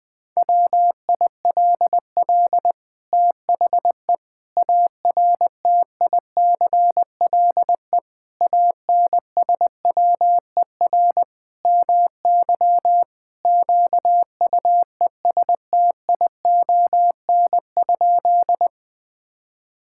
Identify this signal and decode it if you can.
Morse code, heard as a keyed tone, about 21 wpm, 700 Hz. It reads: WILL THE ARTICLE ANSWER MY QUESTION?
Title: Sentences from Top 1000 Words - Review Part 5 - 20wpm